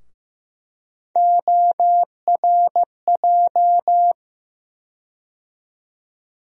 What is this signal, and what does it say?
Morse code, heard as a keyed tone, about 15 wpm, 700 Hz. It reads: ORJ
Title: USA Call Sign Postfixes - 3 Letters 15wpm